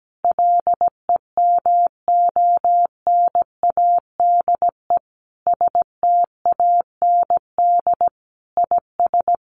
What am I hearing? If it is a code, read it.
Morse code, heard as a keyed tone, about 17 wpm, 700 Hz. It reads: LEMONADE STAND IS
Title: Additional Easy Phrases 17wpm